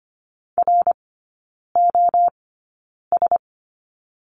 Morse code, heard as a keyed tone, about 25 wpm, spaced as for 9 wpm, 690 Hz. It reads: ROS
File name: Lesson 15e - Number 5 - Three Characters 25wpm